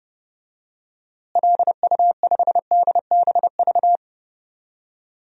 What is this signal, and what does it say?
Morse code, heard as a keyed tone, about 30 wpm, 700 Hz. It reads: LU5DBV
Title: DXCC-100 Call Signs - 6 Characters - Repeated 3x - 30wpm